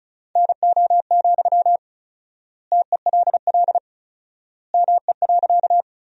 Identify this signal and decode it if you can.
Morse code, heard as a keyed tone, about 34 wpm, 700 Hz. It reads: NO, TELL ME.
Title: 4x Word-Spacing - Sentences from Top 200 Words - 35wpm